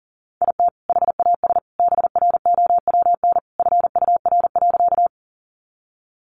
Morse code, heard as a keyed tone, about 40 wpm, 700 Hz. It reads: IT HAS BROWN FUR.
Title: Easy Phrases 40wpm